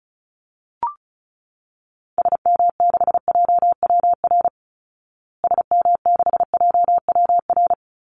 Morse code, heard as a keyed tone, about 34 wpm, 700 Hz. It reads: SM6JWR SM6JWR
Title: Call Signs Repeated 3x - 35wpm